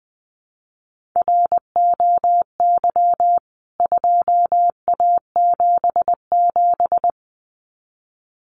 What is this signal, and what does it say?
Morse code, heard as a keyed tone, about 20 wpm, 700 Hz. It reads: ROY 2A77
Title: CWT Exchange 20wpm